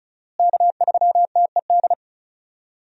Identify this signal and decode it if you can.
Morse code, heard as a keyed tone, about 34 wpm, 700 Hz. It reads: K3TED
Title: USA Call Signs - 5 Characters - Repeated 2x - 35wpm